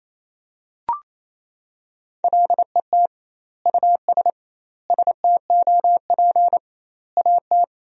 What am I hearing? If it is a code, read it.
Morse code, heard as a keyed tone, about 28 wpm, 700 Hz. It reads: LET US STOP AT
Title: 2x Word-Spacing - Sentences from Top 400 Words - 28wpm